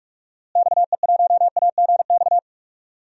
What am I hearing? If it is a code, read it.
Morse code, heard as a keyed tone, about 42 wpm, 700 Hz. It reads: XE1AGX
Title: DXCC-100 Call Signs - 6 Characters - Repeated 2x - 45wpm